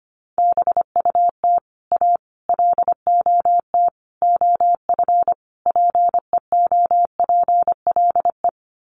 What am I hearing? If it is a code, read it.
Morse code, heard as a keyed tone, about 25 wpm, 700 Hz. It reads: BUT A LOT OF PEOPLE
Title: Sets of 5 Words Encore2 25wpm